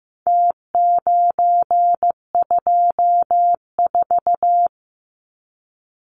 Morse code, heard as a keyed tone, about 15 wpm, 700 Hz. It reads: T924
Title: Numbers - Contest Style - 4-Digit 15wpm